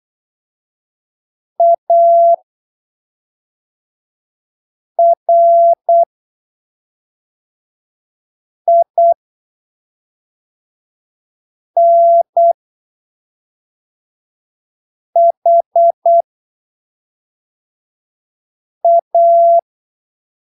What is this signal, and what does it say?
Morse code, heard as a keyed tone, about 8 wpm, spaced as for 3 wpm, 680 Hz. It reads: ARINHA
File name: arquivos / texto / texto 14 08ppm